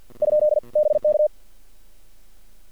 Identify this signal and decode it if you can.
Morse code, heard as a keyed tone, about 51 wpm, 600 Hz. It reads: ? BK
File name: archived / CRSnet